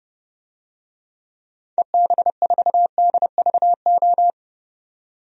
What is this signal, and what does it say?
Morse code, heard as a keyed tone, about 31 wpm, 700 Hz. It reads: EB4DVO
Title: DXCC-100 Call Signs - 6 Characters - 30wpm